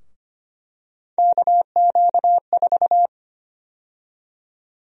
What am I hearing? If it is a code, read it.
Morse code, heard as a keyed tone, about 25 wpm, 700 Hz. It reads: KQ4
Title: USA Call Sign Prefixes - Repeated 2x - 25wpm